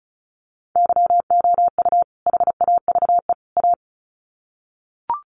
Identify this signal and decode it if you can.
Morse code, heard as a keyed tone, about 34 wpm, 700 Hz. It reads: YOU HAVE A
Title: Sets of 3 Words 35wpm